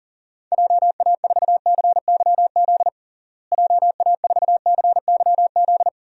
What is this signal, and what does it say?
Morse code, heard as a keyed tone, about 40 wpm, 700 Hz. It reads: JA4CYZ JA4CYZ
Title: DXCC-100 Call Signs - 6 Characters - Repeated 3x - 40wpm